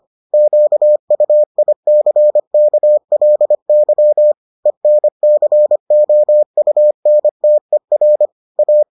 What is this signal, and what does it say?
Morse code, heard as a keyed tone, about 25 wpm, 600 Hz. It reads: QUICKLY ENCOUNTER A